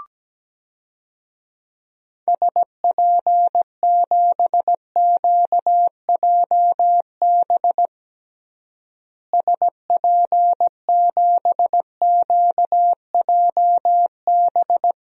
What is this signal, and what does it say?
Morse code, heard as a keyed tone, about 17 wpm, 700 Hz. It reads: SP7QJB SP7QJB
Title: DXCC-100 Call Signs - 6 Characters - Repeated 3x - 17wpm